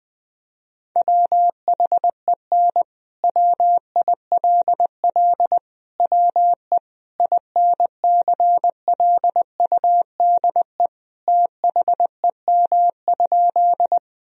Morse code, heard as a keyed tone, about 21 wpm, 700 Hz. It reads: WHEN WILL WE INCLUDE THEM?
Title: Sentences from Top 1000 Words - Review Part 3 - 20wpm